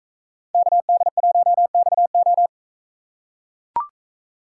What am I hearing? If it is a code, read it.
Morse code, heard as a keyed tone, about 44 wpm, 700 Hz. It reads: KD1XO